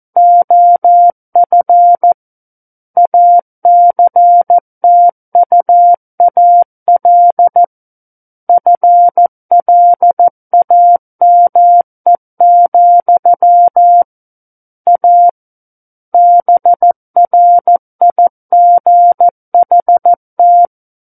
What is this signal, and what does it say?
Morse code, heard as a keyed tone, about 14 wpm, 700 Hz. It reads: OF ACTUAL FLAME, A BRIGHT
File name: Book One / 05-Chapter 5 - 14 WPM